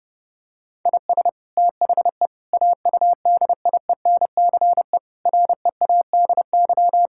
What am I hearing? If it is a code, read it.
Morse code, heard as a keyed tone, about 30 wpm, 700 Hz. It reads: IS THE AUDIENCE READY
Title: Sentences from Top 1000 Words - Review Part 5 - 30wpm